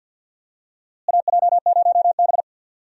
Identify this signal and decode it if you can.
Morse code, heard as a keyed tone, about 52 wpm, 700 Hz. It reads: AJ0B
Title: USA Call Signs - 4 Characters - Repeated 3x - 50wpm